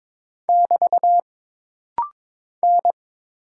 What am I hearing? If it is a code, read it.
Morse code, heard as a keyed tone, about 22 wpm, 700 Hz. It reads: <BT> N
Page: This rapid-fire timing is geared to push your ICR (Instant Character Recognition) skills.